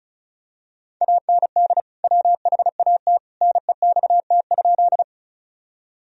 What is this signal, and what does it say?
Morse code, heard as a keyed tone, about 36 wpm, 700 Hz. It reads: AND WHAT NEXT?
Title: Sentences from Top 1000 Words - Review Part 1 - 35wpm